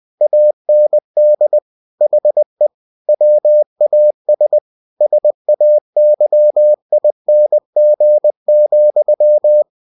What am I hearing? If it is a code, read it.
Morse code, heard as a keyed tone, about 20 wpm, 600 Hz. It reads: AND HE WAS SAYING,
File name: cw-books / winnie-the-pooh / winnie-20wpm-page-0070